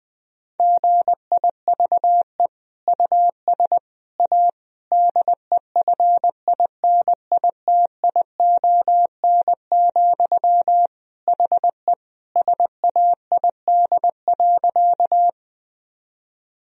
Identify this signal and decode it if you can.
Morse code, heard as a keyed tone, about 21 wpm, 700 Hz. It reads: GIVE US A DEFINITION, HE SAID.
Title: Sentences from Top 1000 Words - Review Part 4 - 20wpm